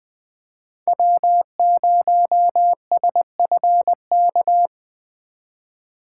Morse code, heard as a keyed tone, about 20 wpm, 700 Hz. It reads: W0SFK
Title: USA Call Signs - 5 Characters - 4x Character Spacing - 20wpm